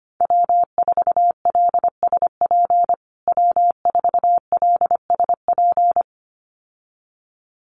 Morse code, heard as a keyed tone, about 25 wpm, 700 Hz. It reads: W4LSP W4LSP
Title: USA Call Signs Repeated 2x 25wpm